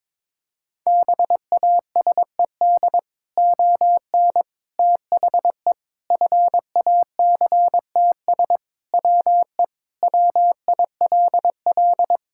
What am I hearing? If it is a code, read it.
Morse code, heard as a keyed tone, about 21 wpm, 700 Hz. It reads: BASED ON THE FACTS WE WILL
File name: practice / 22wpm / Sentences with 3-Word Phrases - 22wpm